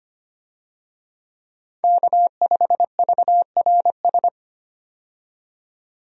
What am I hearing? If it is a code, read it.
Morse code, heard as a keyed tone, about 25 wpm, 700 Hz. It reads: K5VRS
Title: USA Call Signs - 5 Characters - 25wpm